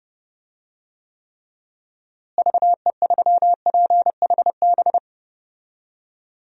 Morse code, heard as a keyed tone, about 32 wpm, 700 Hz. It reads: VE3PHB
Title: DXCC-100 Call Signs - 6 Characters - Repeated 3x - 30wpm